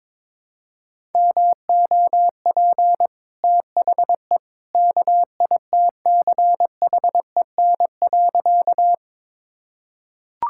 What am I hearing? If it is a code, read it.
Morse code, heard as a keyed tone, about 22 wpm, 700 Hz. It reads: MOP THE KITCHEN.
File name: practice / 22wpm / Easy Sentences with Hints - 22wpm